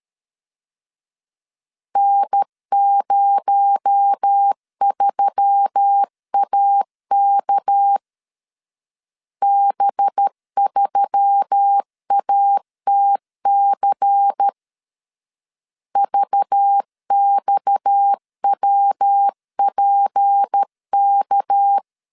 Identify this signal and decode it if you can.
Morse code, heard as a keyed tone, about 13 wpm, 800 Hz. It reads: N03AK B3ATC VXWPK
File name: CW / DET / 1 6 1500C S B 12wpm 3431